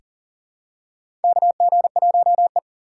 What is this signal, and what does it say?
Morse code, heard as a keyed tone, about 40 wpm, 700 Hz. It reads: KG1E
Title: USA Call Signs - 4 Characters - 4x Character Spacing - 40wpm